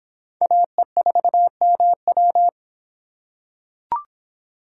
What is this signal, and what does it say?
Morse code, heard as a keyed tone, about 25 wpm, 700 Hz. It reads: AE4MW